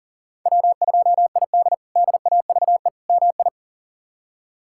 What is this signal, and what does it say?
Morse code, heard as a keyed tone, about 40 wpm, 700 Hz. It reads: W2ID DAVE MI
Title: K1USN SST QSO Practice - 40wpm